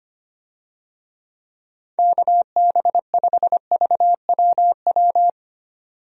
Morse code, heard as a keyed tone, about 25 wpm, 700 Hz. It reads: KB5VWW
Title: USA Call Signs - 6 Characters - 25wpm